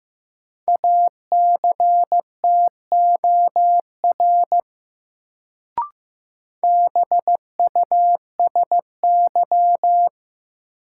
Morse code, heard as a keyed tone, about 15 wpm, 700 Hz. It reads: ACTOR BUSY
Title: Sentences from Top 1500 Words - New Word Drill Alt - 15wpm